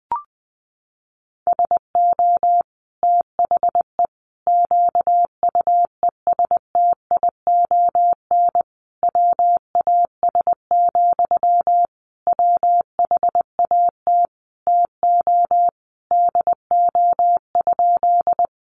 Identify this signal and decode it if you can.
Morse code, heard as a keyed tone, about 20 wpm, 700 Hz. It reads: SO THE QUESTION WAS, WHAT TO DO?
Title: Sentences from Top 400 Words 20wpm